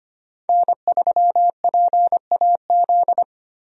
Morse code, heard as a keyed tone, about 25 wpm, 700 Hz. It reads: N3PAZ
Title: USA Call Signs - 5 Characters - 25wpm